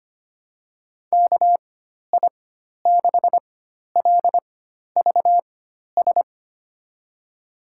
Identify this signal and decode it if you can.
Morse code, heard as a keyed tone, about 25 wpm, spaced as for 12 wpm, 700 Hz. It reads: KI6LVS
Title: USA Call Signs - 6 Characters - 4x Character Spacing - 25wpm